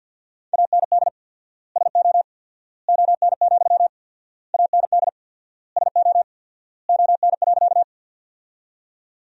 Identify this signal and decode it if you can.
Morse code, heard as a keyed tone, about 46 wpm, 700 Hz. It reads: AND SO ON, AND SO ON.
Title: 4x Word-Spacing - Sentences from Top 100 Words - 50wpm